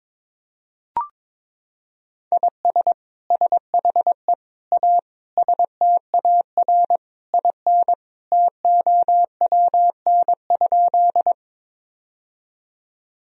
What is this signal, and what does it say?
Morse code, heard as a keyed tone, about 21 wpm, 700 Hz. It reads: IS SHE A STAR IN TOWN?
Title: Sentences from Top 1000 Words - Review Part 2 - 22wpm